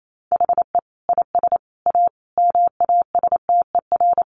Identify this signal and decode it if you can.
Morse code, heard as a keyed tone, about 28 wpm, 700 Hz. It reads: HE IS A MASTER
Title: Sentences from Top 1300 Words 28wpm